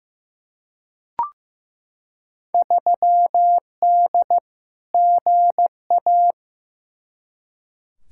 Morse code, heard as a keyed tone, about 15 wpm, 700 Hz. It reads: 3D GA
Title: Field Day Exchanges Encore - 15wpm